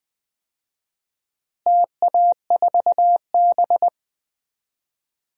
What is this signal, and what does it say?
Morse code, heard as a keyed tone, about 20 wpm, 700 Hz. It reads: TA4B
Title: DXCC-100 Call Signs - 4 Characters - Repeated 2x - 20wpm